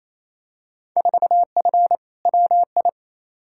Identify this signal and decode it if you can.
Morse code, heard as a keyed tone, about 28 wpm, 700 Hz. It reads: 4F WI
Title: Field Day Exchanges Encore - 28wpm